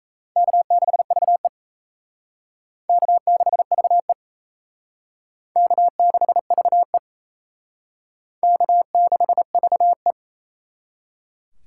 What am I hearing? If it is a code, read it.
Morse code, heard as a keyed tone, about 35 wpm, 700 Hz. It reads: K6VE K6VE K6VE K6VE